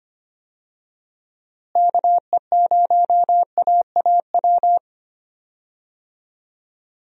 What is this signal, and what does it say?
Morse code, heard as a keyed tone, about 25 wpm, 700 Hz. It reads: KE0AAW
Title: USA Call Signs - 6 Characters - Repeated 2x - 25wpm